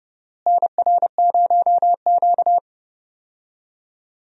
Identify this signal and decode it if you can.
Morse code, heard as a keyed tone, about 30 wpm, 700 Hz. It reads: NR0Q
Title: USA Call Signs - 4 Characters - Repeated 2x - 30wpm